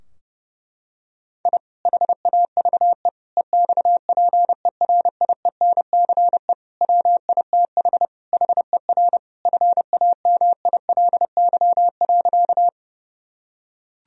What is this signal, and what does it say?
Morse code, heard as a keyed tone, about 32 wpm, 700 Hz. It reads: I HAVE EXPERIENCE WITH HER FAMILY.
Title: Sentences from Top 1000 Words - Review Part 3 - 30wpm